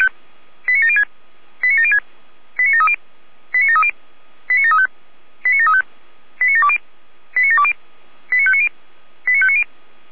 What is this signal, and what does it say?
Five-tone selective call (ZVEI), heard as quick runs of five tones, tones of about 70 ms. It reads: 89706 89706 89730 89730 89735 89735 89720 89720 89590 89590